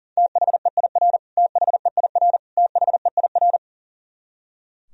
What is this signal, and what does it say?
Morse code, heard as a keyed tone, about 40 wpm, 700 Hz. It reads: THEIR THEIR THEIR
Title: Top 100 Words Repeated 40wpm